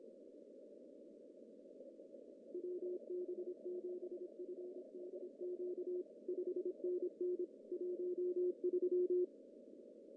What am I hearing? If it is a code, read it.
Morse code, heard as a keyed tone, about 26 wpm, 410 Hz. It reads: WB7FDQ 5NN 13
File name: CW / cqww-cw-k5tr